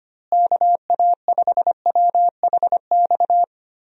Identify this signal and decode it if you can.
Morse code, heard as a keyed tone, about 25 wpm, 700 Hz. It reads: KA5WHX
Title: USA Call Signs - 6 Characters - Repeated 3x - 25wpm